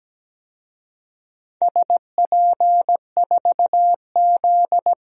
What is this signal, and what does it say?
Morse code, heard as a keyed tone, about 17 wpm, 700 Hz. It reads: SP4Z
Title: DXCC-100 Call Signs - 4 Characters - 17wpm